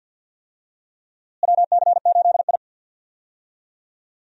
Morse code, heard as a keyed tone, about 47 wpm, 700 Hz. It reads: WC8I
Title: USA Call Signs - 4 Characters - Repeated 2x - 50wpm